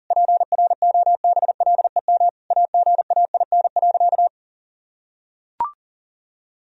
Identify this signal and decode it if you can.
Morse code, heard as a keyed tone, about 40 wpm, 700 Hz. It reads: PROBLEM AGAIN.
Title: Sentences from Top 1000 Words - Review Part 3 - 40wpm